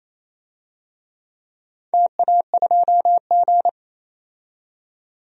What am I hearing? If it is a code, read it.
Morse code, heard as a keyed tone, about 27 wpm, 700 Hz. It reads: TA2G
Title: DXCC-100 Call Signs - 4 Characters - 28wpm